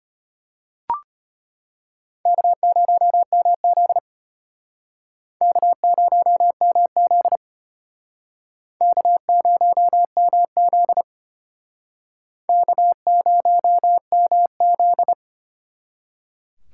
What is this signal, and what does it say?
Morse code, heard as a keyed tone, about 34 wpm, 700 Hz. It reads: K0MZ K0MZ K0MZ K0MZ